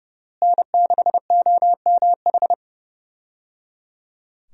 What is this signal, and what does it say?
Morse code, heard as a keyed tone, about 30 wpm, 700 Hz. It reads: N6OMH
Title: USA Call Signs - 5 Characters - Repeated 3x - 30wpm